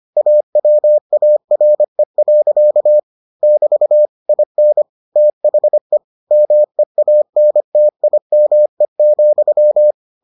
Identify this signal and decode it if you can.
Morse code, heard as a keyed tone, about 25 wpm, 600 Hz. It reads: AWARE. <BT> IN THE MEANTIME,